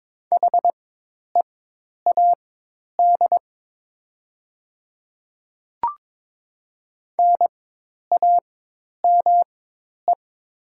Morse code, heard as a keyed tone, about 22 wpm, spaced as for 10 wpm, 690 Hz. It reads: HEAD NAME
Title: 4x Character Spacing - Top 300 Words - 22wpm